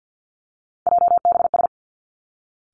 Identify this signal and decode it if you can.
Morse code, heard as a keyed tone, about 50 wpm, 700 Hz. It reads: JBS
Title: Three Challenging Letter-Number Combinations 50wpm